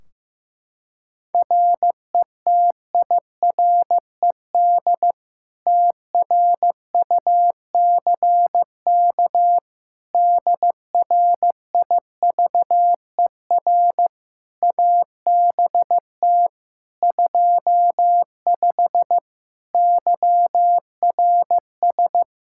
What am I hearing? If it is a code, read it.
Morse code, heard as a keyed tone, about 15 wpm, 700 Hz. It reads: RETIRED TRUCK DRIVER ABT 25 YRS
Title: Practice QSO - Exchange 3 - More Info - 15wpm